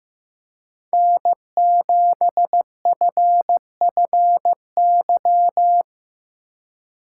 Morse code, heard as a keyed tone, about 15 wpm, 700 Hz. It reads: N7FFY
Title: USA Call Signs - 5 Characters - 4x Character Spacing - 15wpm